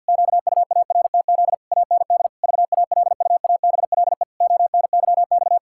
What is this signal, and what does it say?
Morse code, heard as a keyed tone, about 51 wpm, 700 Hz. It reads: QUARTZ AND VALUABLE ONYX
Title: Pangram Challenge v2 - 50wpm